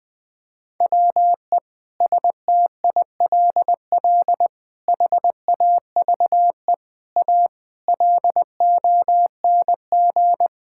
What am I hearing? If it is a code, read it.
Morse code, heard as a keyed tone, about 21 wpm, 700 Hz. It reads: WE STILL HAVE A LONG
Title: Sentences with 6-Word Phrases - 20wpm